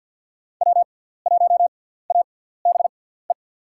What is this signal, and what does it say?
Morse code, heard as a keyed tone, about 53 wpm, spaced as for 17 wpm, 700 Hz. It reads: W1ABE
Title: USA Call Signs - 5 Characters - 6x Character Spacing - 50wpm